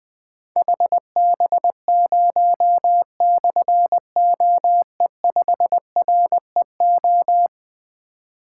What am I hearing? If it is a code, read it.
Morse code, heard as a keyed tone, about 21 wpm, 700 Hz. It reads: HB0/OE5REO
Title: Difficult Call Signs 20wpm